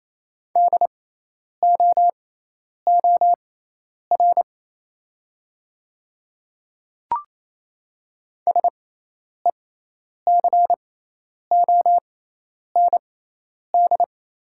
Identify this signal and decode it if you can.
Morse code, heard as a keyed tone, about 27 wpm, spaced as for 10 wpm, 690 Hz. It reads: DOOR SECOND
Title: 6x Character Spacing - Top 300 Words - 28wpm